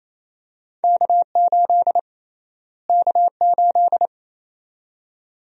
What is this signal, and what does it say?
Morse code, heard as a keyed tone, about 28 wpm, 700 Hz. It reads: K8 K8
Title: USA Call Sign Prefixes - Repeated 2x - 28wpm